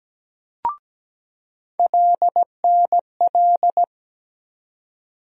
Morse code, heard as a keyed tone, about 17 wpm, 700 Hz. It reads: LNL
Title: USA Call Sign Postfixes - 3 Letters 17wpm